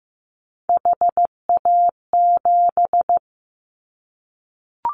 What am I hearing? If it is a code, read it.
Morse code, heard as a keyed tone, about 15 wpm, 700 Hz. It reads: HA7
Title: Callsign Prefixes for Top 50 DXCC - 15wpm